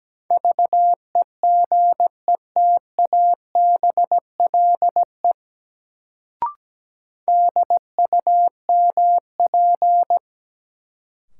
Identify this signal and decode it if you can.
Morse code, heard as a keyed tone, about 17 wpm, 700 Hz. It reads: VEGETABLE DUMP
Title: Sentences from Top 1300 Words - New Word Drill - 17wpm